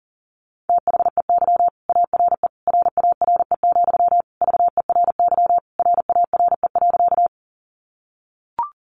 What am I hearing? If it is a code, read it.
Morse code, heard as a keyed tone, about 40 wpm, 700 Hz. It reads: THEY ARE RARE, VERY RARE.
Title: Sentences from Top 1100 Words 40wpm